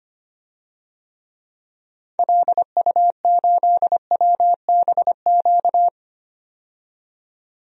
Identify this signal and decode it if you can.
Morse code, heard as a keyed tone, about 25 wpm, 700 Hz. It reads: LU8WBQ
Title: DXCC-100 Call Signs - 6 Characters - 25wpm